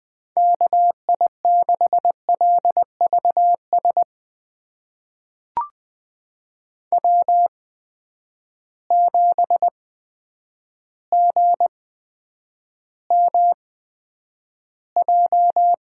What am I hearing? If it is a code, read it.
Morse code, heard as a keyed tone, about 20 wpm, spaced as for 9 wpm, 700 Hz. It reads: KI6LVS W7GMJ
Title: USA Call Signs - 8x Character Spacing - 20wpm